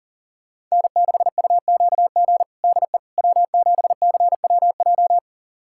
Morse code, heard as a keyed tone, about 40 wpm, 700 Hz. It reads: N6UQG DE W7CWJ
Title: USA Callsign DE Callsign - 5 Characters - 40wpm